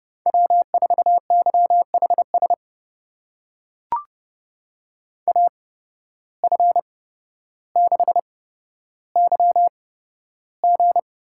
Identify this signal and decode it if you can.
Morse code, heard as a keyed tone, about 30 wpm, spaced as for 13 wpm, 700 Hz. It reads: W4YHS AF6YG
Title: USA Call Signs - 5 Characters - 8x Character Spacing - 30wpm